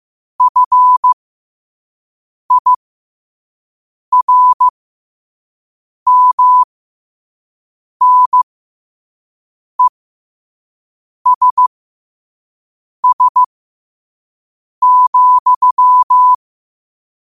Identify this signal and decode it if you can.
Morse code, heard as a keyed tone, about 15 wpm, spaced as for 5 wpm, 1000 Hz. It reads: FIRMNESS,
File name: qotdfiles / 5wpm